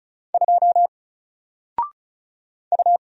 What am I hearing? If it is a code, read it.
Morse code, heard as a keyed tone, about 33 wpm, 700 Hz. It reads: 2 U
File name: Single-Letter-Number-35wpm